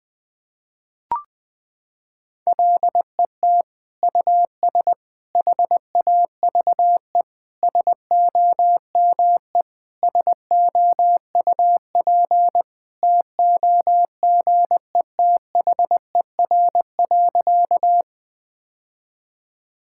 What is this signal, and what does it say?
Morse code, heard as a keyed tone, about 20 wpm, 700 Hz. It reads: LET US HAVE SOME SOUP TOGETHER.
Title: Sentences from Top 1000 Words - Review Part 6 - 20wpm